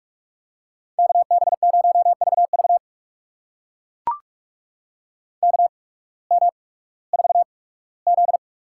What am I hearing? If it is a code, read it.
Morse code, heard as a keyed tone, about 42 wpm, spaced as for 20 wpm, 700 Hz. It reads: KB0VV KM4Z
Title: USA Call Signs - 8x Character Spacing - 45wpm